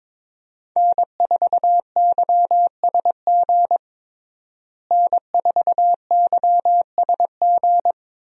Morse code, heard as a keyed tone, about 22 wpm, 700 Hz. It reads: N4YSG N4YSG
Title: USA Call Signs - 5 Characters - Repeated 3x - 22wpm